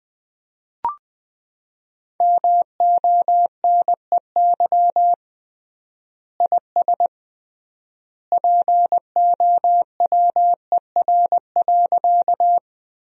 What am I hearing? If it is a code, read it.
Morse code, heard as a keyed tone, about 20 wpm, 700 Hz. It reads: MONEY IS POWER.
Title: 3x Word-Spacing - Sentences from Top 500 Words - 20wpm